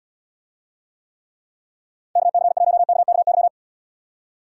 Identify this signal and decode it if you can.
Morse code, heard as a keyed tone, about 78 wpm, 700 Hz. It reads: KC1KKY